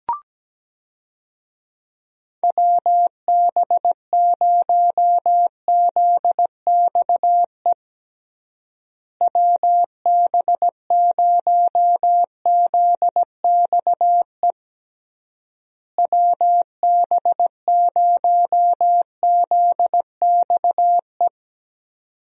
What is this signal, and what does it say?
Morse code, heard as a keyed tone, about 17 wpm, 700 Hz. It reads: WB0ZXE WB0ZXE WB0ZXE
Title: USA Call Signs - 6 Characters - Repeated 3x - 17wpm